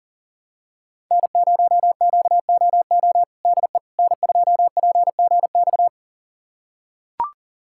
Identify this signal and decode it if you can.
Morse code, heard as a keyed tone, about 40 wpm, 700 Hz. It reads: N0QOO DE N2PGX
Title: USA Callsign DE Callsign - 5 Characters - 40wpm